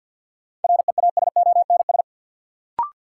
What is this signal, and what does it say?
Morse code, heard as a keyed tone, about 47 wpm, 700 Hz. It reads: REASONS
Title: Sets of 4 Words - New Word Drill - 50wpm